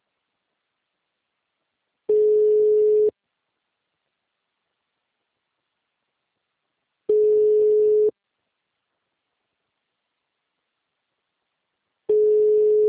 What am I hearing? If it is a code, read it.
Ringback tone (Europe, 425 Hz).